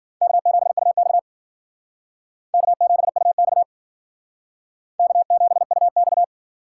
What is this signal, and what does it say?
Morse code, heard as a keyed tone, about 52 wpm, 700 Hz. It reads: K7UX K7UX K7UX